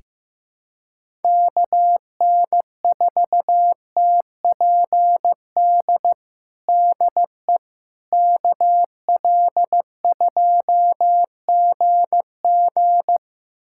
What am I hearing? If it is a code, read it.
Morse code, heard as a keyed tone, about 15 wpm, 700 Hz. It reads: KN4TPD DE KL2GG
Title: USA Callsign DE Callsign 15wpm